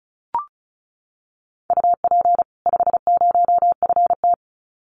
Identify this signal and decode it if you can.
Morse code, heard as a keyed tone, about 34 wpm, 700 Hz. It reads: UP 50FT
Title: Common QSO Phrases 35wpm